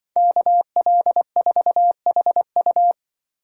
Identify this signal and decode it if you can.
Morse code, heard as a keyed tone, about 23 wpm, 700 Hz. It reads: KL4HU
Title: Speed Racing - USA 2x2 Call Signs 20wpm